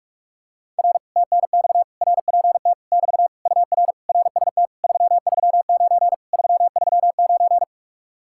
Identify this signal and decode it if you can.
Morse code, heard as a keyed tone, about 49 wpm, 700 Hz. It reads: R TNX RPT <BT> UR RST 339 339